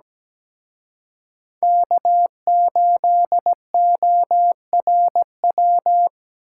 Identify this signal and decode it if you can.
Morse code, heard as a keyed tone, about 20 wpm, 700 Hz. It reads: K8ORW
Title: Speed Racing - USA 1x3 Call Signs 17wpm